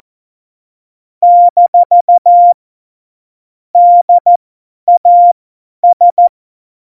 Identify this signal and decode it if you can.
Morse code, heard as a keyed tone, about 14 wpm, spaced as for 10 wpm, 700 Hz. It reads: - DAS
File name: Morsekurs / Witze / Witz 09. Januar 2018, 50 F70